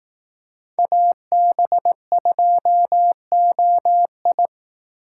Morse code, heard as a keyed tone, about 18 wpm, 700 Hz. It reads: AB2OI
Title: Speed Racing - USA 2x2 Call Signs 15wpm